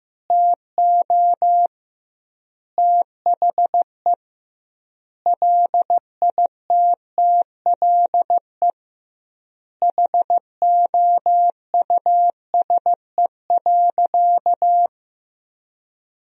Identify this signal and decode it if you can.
Morse code, heard as a keyed tone, about 15 wpm, 700 Hz. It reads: TO THE LITTLE HOUSE.
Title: 2x Word-Spacing - Sentences from Top 200 Words - 15wpm